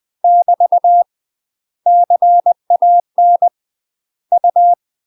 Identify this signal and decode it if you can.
Morse code, heard as a keyed tone, about 20 wpm, 700 Hz. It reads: <BT> CAN U
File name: MASTER QSOs TRAINING/20WPM